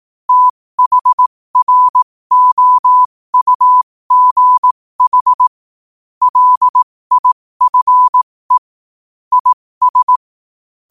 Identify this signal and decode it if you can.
Morse code, heard as a keyed tone, about 18 wpm, spaced as for 15 wpm, 1000 Hz. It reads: THROUGH LIFE IS
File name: qotdfiles / 15wpm